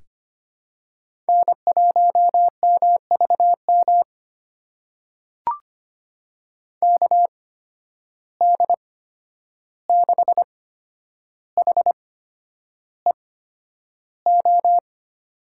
Morse code, heard as a keyed tone, about 25 wpm, spaced as for 10 wpm, 700 Hz. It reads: N1MVM KD6HEO
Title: USA Call Signs - 8x Character Spacing - 25wpm